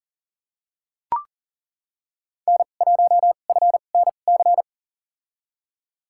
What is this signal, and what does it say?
Morse code, heard as a keyed tone, about 37 wpm, spaced as for 28 wpm, 700 Hz. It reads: N1FNC
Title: USA Call Signs - 5 Characters - 2x Character Spacing - 40wpm